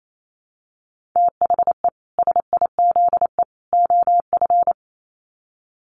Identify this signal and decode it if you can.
Morse code, heard as a keyed tone, about 28 wpm, 700 Hz. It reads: THE SIZE OF
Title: Sets of 3 Words 28wpm